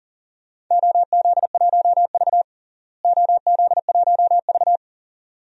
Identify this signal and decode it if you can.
Morse code, heard as a keyed tone, about 40 wpm, 700 Hz. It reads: OZ1V OZ1V
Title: DXCC-100 Call Signs - 4 Characters - Repeated 2x - 40wpm